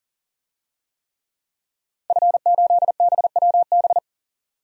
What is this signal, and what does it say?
Morse code, heard as a keyed tone, about 40 wpm, 700 Hz. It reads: F8BWB
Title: DXCC-100 Call Signs - 5 Characters - 40wpm